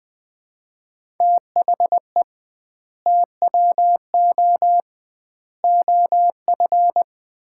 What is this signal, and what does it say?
Morse code, heard as a keyed tone, about 21 wpm, 700 Hz. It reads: THE TWO OF
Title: 2x Word-Spacing - Sets of 4 Words - 20wpm